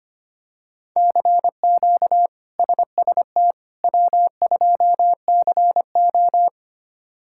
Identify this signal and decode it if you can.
Morse code, heard as a keyed tone, about 25 wpm, 700 Hz. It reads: CQ SST W2CO
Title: K1USN SST QSO Practice - 25wpm